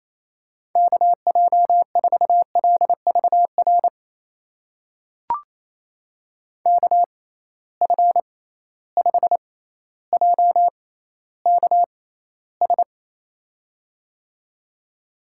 Morse code, heard as a keyed tone, about 28 wpm, spaced as for 14 wpm, 700 Hz. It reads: KJ4LVR KF5JKS
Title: USA Call Signs - 6 Characters - 6x Character Spacing - 28wpm